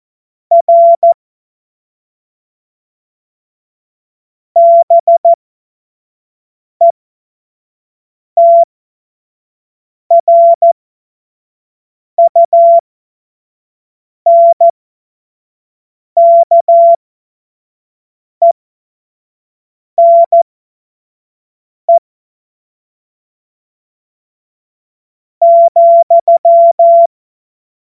Morse code, heard as a keyed tone, about 14 wpm, spaced as for 5 wpm, 680 Hz. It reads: R BETRUNKENE,
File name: Morsekurs / Witze / Witz 01. Juni 2021, 25 F70